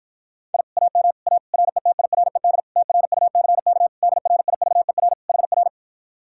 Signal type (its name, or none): Morse code